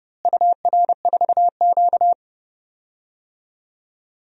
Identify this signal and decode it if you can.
Morse code, heard as a keyed tone, about 30 wpm, 700 Hz. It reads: UR4Q
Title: DXCC-100 Call Signs - 4 Characters - Repeated 2x - 30wpm